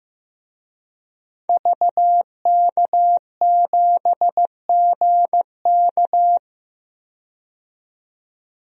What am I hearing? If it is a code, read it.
Morse code, heard as a keyed tone, about 15 wpm, 700 Hz. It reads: VK7GK
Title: DXCC-100 Call Signs - 5 Characters - 15wpm